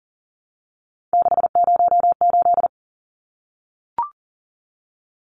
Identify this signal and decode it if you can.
Morse code, heard as a keyed tone, about 40 wpm, 700 Hz. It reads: 608
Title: Numbers - 3-Digit 40wpm